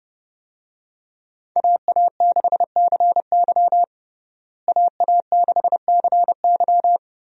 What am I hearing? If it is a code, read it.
Morse code, heard as a keyed tone, about 32 wpm, 700 Hz. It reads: AA6CY AA6CY
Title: USA Call Signs - 5 Characters - Repeated 3x - 30wpm